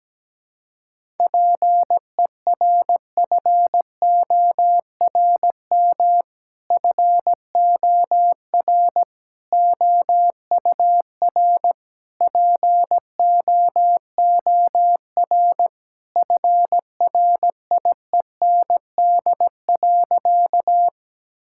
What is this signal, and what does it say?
Morse code, heard as a keyed tone, about 17 wpm, 700 Hz. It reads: PERFORM FOR OUR POOR FRIEND.
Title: Sentences from Top 1000 Words - Review Part 5 - 17wpm